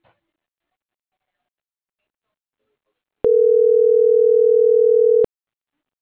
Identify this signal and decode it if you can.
Ringback tone (North America), 440 + 480 Hz.